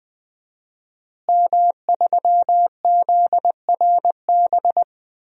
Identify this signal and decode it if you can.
Morse code, heard as a keyed tone, about 20 wpm, 700 Hz. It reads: M3ZRB
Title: DXCC-100 Call Signs - 5 Characters - Repeated 3x - 20wpm